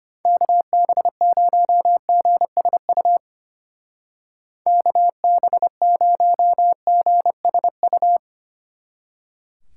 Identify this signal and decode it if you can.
Morse code, heard as a keyed tone, about 30 wpm, 700 Hz. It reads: KB0GSU KB0GSU
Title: Speed Racing - USA 2x3 Call Signs 25wpm